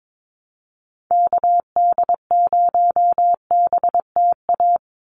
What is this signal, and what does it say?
Morse code, heard as a keyed tone, about 22 wpm, 700 Hz. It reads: KD0BTA
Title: USA Call Signs 22wpm